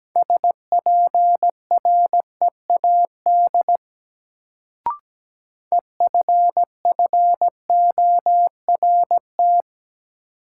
Morse code, heard as a keyed tone, about 17 wpm, 700 Hz. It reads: SPREAD EFFORT
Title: Sentences from Top 1200 Words - New Word Drill Alt - 17wpm